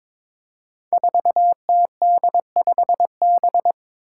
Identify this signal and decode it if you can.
Morse code, heard as a keyed tone, about 22 wpm, 700 Hz. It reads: 4TD5B
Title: Five Challenging Letter-Number Combinations 22wpm